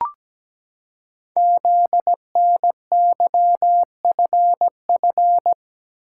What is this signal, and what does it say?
Morse code, heard as a keyed tone, about 17 wpm, 700 Hz. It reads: ZNYFF
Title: Code Groups - 5 Letters - 17wpm